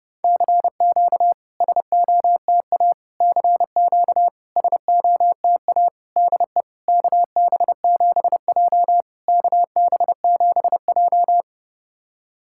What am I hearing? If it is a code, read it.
Morse code, heard as a keyed tone, about 31 wpm, 690 Hz. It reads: CQ SOTA CQ SOTA DE KB7J KB7J
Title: SOTA QSO Practice 30wpm